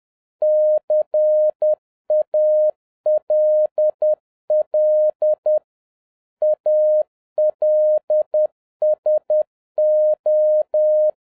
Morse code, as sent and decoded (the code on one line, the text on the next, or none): -.-. .- .-.. .-.. / .- .-.. ... ---
CALL ALSO